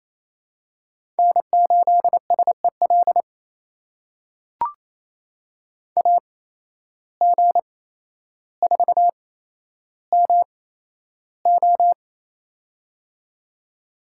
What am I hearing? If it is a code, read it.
Morse code, heard as a keyed tone, about 27 wpm, spaced as for 12 wpm, 700 Hz. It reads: N8SEL AG4MO
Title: USA Call Signs - 8x Character Spacing - 28wpm